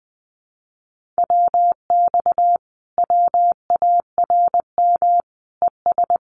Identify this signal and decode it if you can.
Morse code, heard as a keyed tone, about 20 wpm, 700 Hz. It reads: WX WARM ES
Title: Common QSO Phrases Encore 20wpm